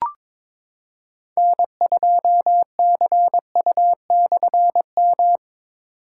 Morse code, heard as a keyed tone, about 22 wpm, 700 Hz. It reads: N2CU/M
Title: K1USN SST QSO Practice - 22wpm